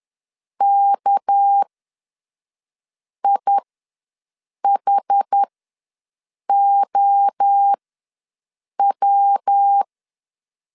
Morse code, heard as a keyed tone, about 10 wpm, spaced as for 5 wpm, 800 Hz. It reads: KIHOW
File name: LEZIONI / 7 GF9Z / 7 GF9 ws 10wpm 4034